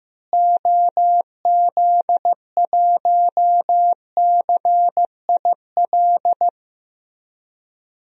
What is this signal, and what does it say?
Morse code, heard as a keyed tone, about 15 wpm, 700 Hz. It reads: OZ1CIL
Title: DXCC-100 Call Signs - 6 Characters - Repeated 2x - 15wpm